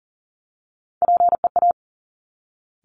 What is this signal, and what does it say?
Morse code, heard as a keyed tone, about 41 wpm, 700 Hz. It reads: PEA
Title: Top Three Letter Combinations in Words 40wpm